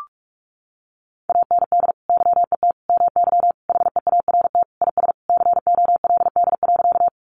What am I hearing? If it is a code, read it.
Morse code, heard as a keyed tone, about 44 wpm, 700 Hz. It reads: AND YET MY HEART IS COLD.
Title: Sentences from Top 1200 Words 45wpm